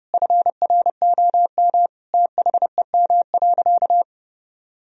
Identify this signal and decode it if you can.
Morse code, heard as a keyed tone, about 30 wpm, 700 Hz. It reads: FROM THEM.
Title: Sentences from Top 1000 Words - Review Part 4 - 30wpm